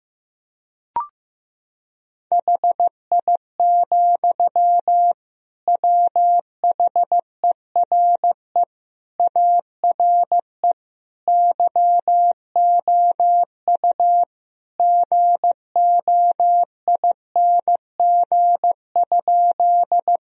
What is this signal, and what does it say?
Morse code, heard as a keyed tone, about 15 wpm, 700 Hz. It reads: HI, WHERE ARE YOU GOING?
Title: Sentences from Top 1000 Words - Review Part 3 - 15wpm